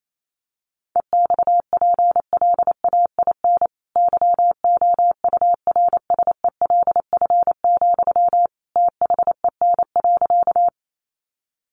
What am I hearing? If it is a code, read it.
Morse code, heard as a keyed tone, about 28 wpm, 700 Hz. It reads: EXPLAIN YOURSELF, THEN.
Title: Sentences from Top 1100 Words 28wpm